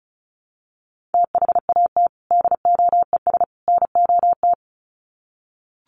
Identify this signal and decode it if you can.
Morse code, heard as a keyed tone, about 34 wpm, 700 Hz. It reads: THAT DOES NOT
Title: Sets of 3 Words 35wpm